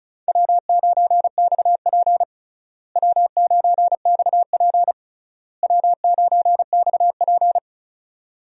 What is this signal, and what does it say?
Morse code, heard as a keyed tone, about 34 wpm, 700 Hz. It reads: W9XP W9XP W9XP
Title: USA Call Signs - 4 Characters - Repeated 3x - 35wpm